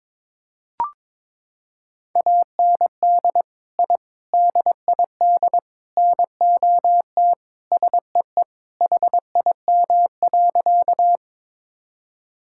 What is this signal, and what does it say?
Morse code, heard as a keyed tone, about 22 wpm, 700 Hz. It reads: AND I DID NOT SEE HIM.
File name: practice / 22wpm / Sentences with 4-Word Phrases - 22wpm